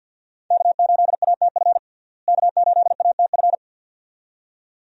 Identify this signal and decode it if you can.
Morse code, heard as a keyed tone, about 50 wpm, 700 Hz. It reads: K8ATF K8ATF
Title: USA Call Signs - 5 Characters - Repeated 2x - 50wpm